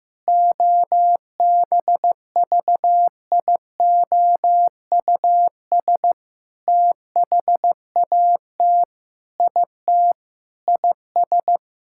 Morse code, heard as a keyed tone, about 16 wpm, 700 Hz. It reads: OBVIOUS THAT IT IS
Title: Sentences from Top 1600 Words 15wpm